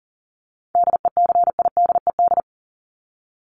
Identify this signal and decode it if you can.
Morse code, heard as a keyed tone, about 40 wpm, 700 Hz. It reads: DECIDED
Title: ED Suffix - Top 100 Words with Base Words 40wpm